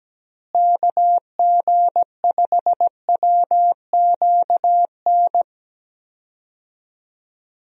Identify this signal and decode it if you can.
Morse code, heard as a keyed tone, about 17 wpm, 700 Hz. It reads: KG5WQN
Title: USA Call Signs - 6 Characters - Repeated 3x - 17wpm